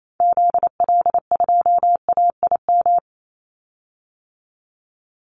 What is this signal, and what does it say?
Morse code, heard as a keyed tone, about 28 wpm, 700 Hz. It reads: ZL2AIM
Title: Call Signs Repeated 3x - 28wpm